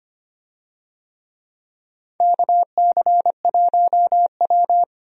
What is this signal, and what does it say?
Morse code, heard as a keyed tone, about 25 wpm, 700 Hz. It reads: KC1W
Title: USA Call Signs - 4 Characters - 25wpm